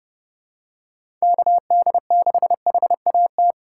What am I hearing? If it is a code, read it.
Morse code, heard as a keyed tone, about 32 wpm, 700 Hz. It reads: KD6HAT
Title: USA Call Signs - 3x Character Spacing - 30wpm